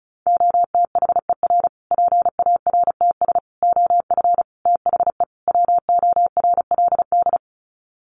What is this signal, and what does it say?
Morse code, heard as a keyed tone, about 34 wpm, 700 Hz. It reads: OTHER PARTS OF THE WORLD
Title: Sets of 5 Words Encore 35wpm